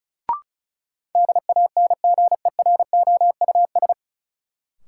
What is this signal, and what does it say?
Morse code, heard as a keyed tone, about 34 wpm, 700 Hz. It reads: DANGEROUS
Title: Sentences from Top 1200 Words - New Word Drill - 35wpm